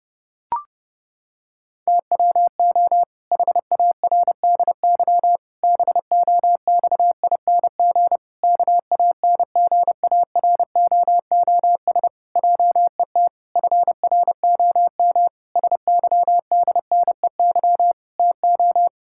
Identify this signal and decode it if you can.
Morse code, heard as a keyed tone, about 30 wpm, 700 Hz. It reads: TWO HARDY BOXING KANGAROOS JET FROM SYDNEY TO
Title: Pangram Challenge v2 - 30wpm